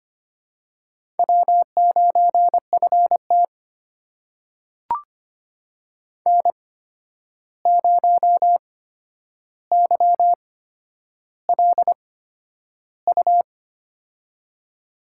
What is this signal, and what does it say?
Morse code, heard as a keyed tone, about 25 wpm, spaced as for 10 wpm, 700 Hz. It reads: W9FT N0YLU
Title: USA Call Signs - 8x Character Spacing - 25wpm